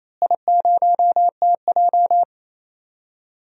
Morse code, heard as a keyed tone, about 28 wpm, 700 Hz. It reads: I0TJ
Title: DXCC-100 Call Signs - 4 Characters - Repeated 3x - 28wpm